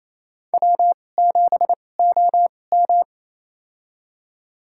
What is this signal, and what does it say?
Morse code, heard as a keyed tone, about 27 wpm, spaced as for 20 wpm, 690 Hz. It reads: W7OM
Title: USA Call Signs - 4 Characters - 2x Character Spacing - 28wpm